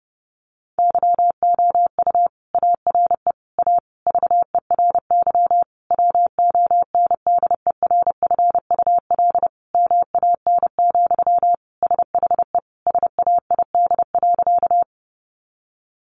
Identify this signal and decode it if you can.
Morse code, heard as a keyed tone, about 30 wpm, 700 Hz. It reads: YOU ARE A VERY WONDERFUL MAN, SHE SAID.
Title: Sentences from Top 1100 Words 30wpm